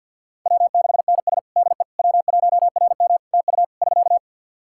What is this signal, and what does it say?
Morse code, heard as a keyed tone, about 52 wpm, 700 Hz. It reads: W6NS DE W1RM TU <SK>
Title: K1USN SST QSO Practice 50wpm